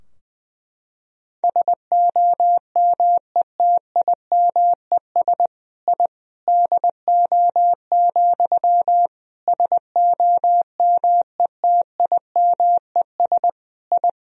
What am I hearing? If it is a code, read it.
Morse code, heard as a keyed tone, about 21 wpm, 700 Hz. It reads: SOMETIMES I DO, SOMETIMES I
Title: Sentences from Top 1000 Words - Review Part 3 - 20wpm